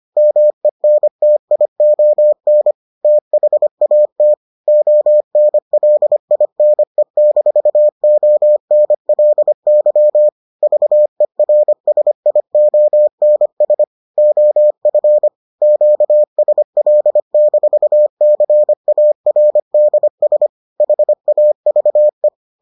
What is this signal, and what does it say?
Morse code, heard as a keyed tone, about 25 wpm, 600 Hz. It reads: MENTION THAT ONLINE-ONLY VERSIONS OF QSL-CARDS HAVE